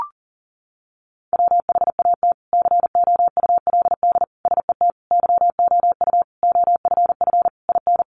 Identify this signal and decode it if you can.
Morse code, heard as a keyed tone, about 40 wpm, 700 Hz. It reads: WHAT COULD SET YOU OFF IN
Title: Sentences from Top 200 Words 40wpm